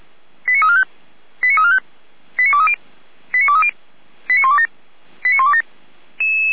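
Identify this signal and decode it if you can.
Five-tone selective call (ZVEI), heard as quick runs of five tones, tones of about 70 ms. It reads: 89345 89345 89230 89230 89127 89127
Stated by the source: ZVEI five-tone selective call received by SDR.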